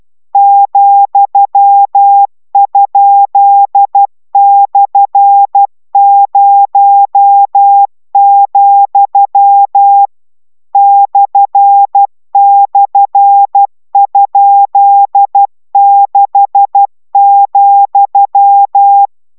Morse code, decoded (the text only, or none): ,?/0, //?6,